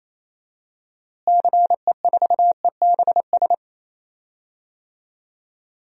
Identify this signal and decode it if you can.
Morse code, heard as a keyed tone, about 28 wpm, 700 Hz. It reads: CE4EBS
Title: DXCC-100 Call Signs - 6 Characters - Repeated 2x - 28wpm